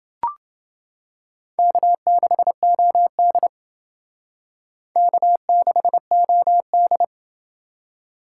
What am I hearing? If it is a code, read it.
Morse code, heard as a keyed tone, about 30 wpm, 700 Hz. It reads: K6OD K6OD